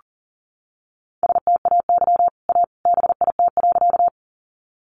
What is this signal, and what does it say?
Morse code, heard as a keyed tone, about 40 wpm, 700 Hz. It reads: STAY A BIT.
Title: Sentences from Top 900 Words 40wpm